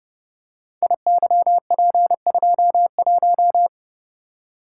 Morse code, heard as a keyed tone, about 28 wpm, 700 Hz. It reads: IYP21
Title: Code Groups - 5 Letter:Numbers - 30wpm